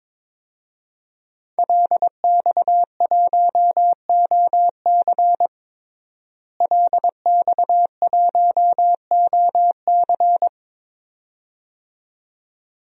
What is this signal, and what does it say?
Morse code, heard as a keyed tone, about 21 wpm, 700 Hz. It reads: LX1OC LX1OC
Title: DXCC-100 Call Signs - 5 Characters - Repeated 2x - 22wpm